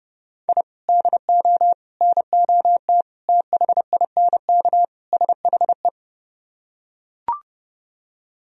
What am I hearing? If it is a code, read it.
Morse code, heard as a keyed tone, about 30 wpm, 700 Hz. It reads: I DO NOT THINK SHE